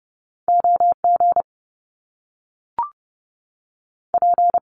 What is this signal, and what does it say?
Morse code, heard as a keyed tone, about 30 wpm, 700 Hz. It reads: O G P
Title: Top Two Letter Combinations in Words 30wpm